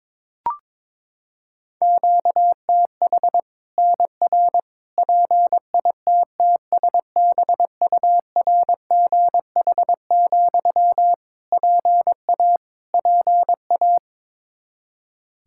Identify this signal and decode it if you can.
Morse code, heard as a keyed tone, about 22 wpm, 700 Hz. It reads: QTH NR PITTSBURGH, PA PA
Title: Practice QSO - Exchange 1 - Beginning - 22wpm